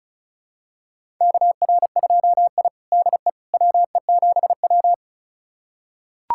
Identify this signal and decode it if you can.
Morse code, heard as a keyed tone, about 36 wpm, 700 Hz. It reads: KR2I DE WE7W
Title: USA Callsign DE Callsign - 4 Characters - 35wpm